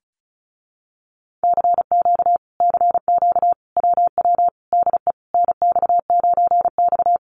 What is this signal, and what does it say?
Morse code, heard as a keyed tone, about 34 wpm, 700 Hz. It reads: CQ CQ WW DE NX9X
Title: Common QSO Phrases Encore 35wpm